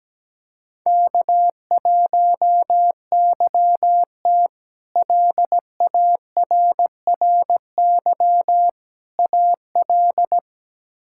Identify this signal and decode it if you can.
Morse code, heard as a keyed tone, about 17 wpm, 700 Hz. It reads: K1YT LARRY AL
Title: K1USN SST QSO Practice - 17wpm